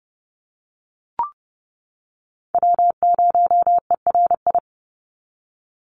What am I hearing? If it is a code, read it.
Morse code, heard as a keyed tone, about 30 wpm, 700 Hz. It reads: W0ERI
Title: NA SOTA Chaser Callsigns 30wpm